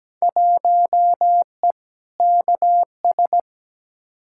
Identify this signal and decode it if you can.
Morse code, heard as a keyed tone, about 17 wpm, 700 Hz. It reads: 1E KS
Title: Field Day Exchanges - 17wpm